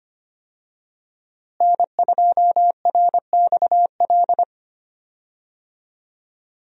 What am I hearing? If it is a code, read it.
Morse code, heard as a keyed tone, about 25 wpm, 700 Hz. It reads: N2RXL
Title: USA Call Signs - 5 Characters - Repeated 2x - 25wpm